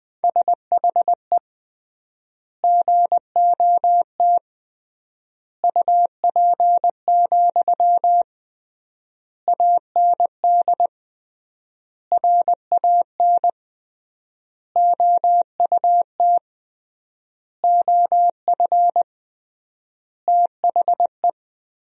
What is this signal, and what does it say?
Morse code, heard as a keyed tone, about 20 wpm, 700 Hz. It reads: SHE GOT UP, AND RAN OUT OF THE
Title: 3x Word-Spacing - Sentences from Top 500 Words - 20wpm